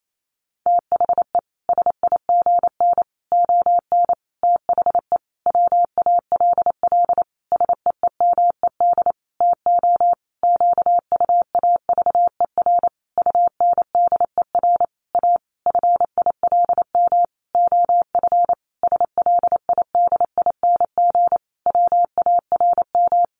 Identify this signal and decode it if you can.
Morse code, heard as a keyed tone, about 28 wpm, 700 Hz. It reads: THE SIGN ON THE WALL SEEMED TO QUAVER UNDER A FILM OF SLIDING WARM
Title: A Sound Of Thunder 28wpm